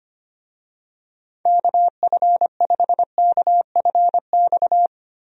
Morse code, heard as a keyed tone, about 25 wpm, 700 Hz. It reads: KF5KFX
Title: USA Call Signs - 6 Characters - Repeated 3x - 25wpm